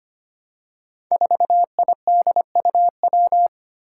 Morse code, heard as a keyed tone, about 25 wpm, 700 Hz. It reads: 4IDUW
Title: Five Challenging Letter-Number Combinations 25wpm